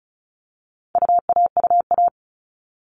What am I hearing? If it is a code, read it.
Morse code, heard as a keyed tone, about 34 wpm, 700 Hz. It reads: UAUA
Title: All Cut Numbers - 4-Digit 35wpm